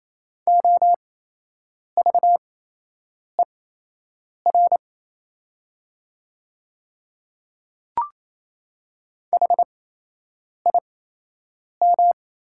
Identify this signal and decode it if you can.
Morse code, heard as a keyed tone, about 27 wpm, spaced as for 8 wpm, 700 Hz. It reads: OVER HIM
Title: 8x Character Spacing - Top 100 Words - 28wpm